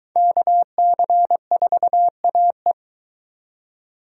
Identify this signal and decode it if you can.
Morse code, heard as a keyed tone, about 22 wpm, 700 Hz. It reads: KC4AE